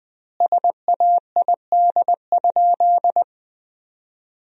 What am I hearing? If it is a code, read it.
Morse code, heard as a keyed tone, about 20 wpm, 700 Hz. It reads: SAID?
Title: 4x Word-Spacing - Sentences from Top 100 Words - 20wpm